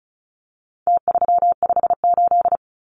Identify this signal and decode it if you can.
Morse code, heard as a keyed tone, about 34 wpm, 700 Hz. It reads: T358
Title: Numbers - Contest Style - 4-Digit 35wpm